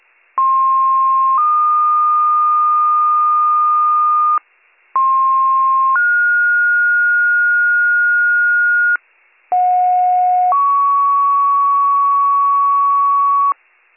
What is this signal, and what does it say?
Two-tone page, heard as a one-second tone, then a three-second tone, about 1060 then 1220 Hz, about 1060 then 1430 Hz, about 730 then 1120 Hz, with noise. It is heard from 0.4 to 13.5 s.